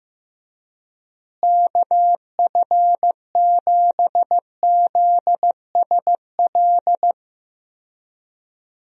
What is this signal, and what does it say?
Morse code, heard as a keyed tone, about 15 wpm, 700 Hz. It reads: KF7ZSL
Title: USA Call Signs - 6 Characters - Repeated 2x - 15wpm